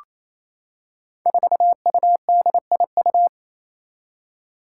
Morse code, heard as a keyed tone, about 27 wpm, 700 Hz. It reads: 4UDIU
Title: Five Challenging Letter-Number Combinations 28wpm